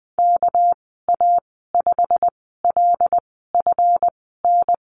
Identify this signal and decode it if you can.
Morse code, heard as a keyed tone, about 20 wpm, spaced as for 15 wpm, 700 Hz. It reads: KA5LFN
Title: 200 Lesson 32g - Letter k - Call Signs 20wpm